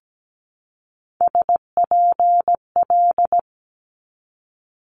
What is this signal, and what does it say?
Morse code, heard as a keyed tone, about 17 wpm, 700 Hz. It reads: SPL
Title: Multi-letter Phonemes 17wpm